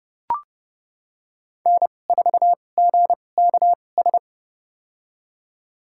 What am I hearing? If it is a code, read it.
Morse code, heard as a keyed tone, about 32 wpm, spaced as for 22 wpm, 700 Hz. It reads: N4GKS
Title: USA Call Signs - 5 Characters - 2x Character Spacing - 30wpm